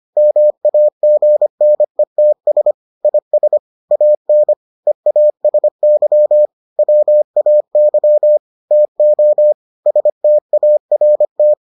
Morse code, heard as a keyed tone, about 25 wpm, 600 Hz. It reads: MAGNETS IS AN EASY WAY TO START